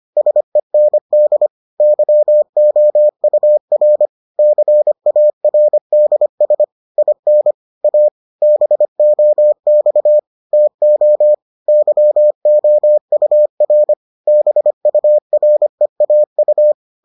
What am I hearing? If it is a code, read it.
Morse code, heard as a keyed tone, about 25 wpm, 600 Hz. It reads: SEND YOUR CARDS IN A BOX TO YOUR BUREAU